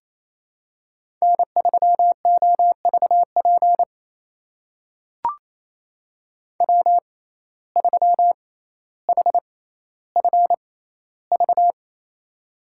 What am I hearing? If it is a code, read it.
Morse code, heard as a keyed tone, about 27 wpm, spaced as for 14 wpm, 700 Hz. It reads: N3OVP W3HFV
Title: USA Call Signs - 6x Character Spacing - 28wpm